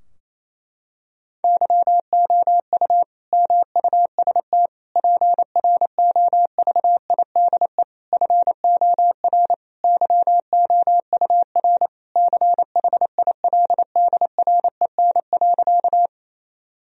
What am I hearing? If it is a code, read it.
Morse code, heard as a keyed tone, about 27 wpm, 700 Hz. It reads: YOU MUST PROVIDE FOR YOUR CHILDREN.
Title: Sentences from Top 1000 Words - Review Part 3 - 28wpm